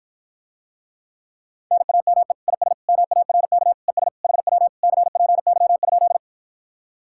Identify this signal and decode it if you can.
Morse code, heard as a keyed tone, about 52 wpm, 700 Hz. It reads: NAME IS MARK ES HW CPY?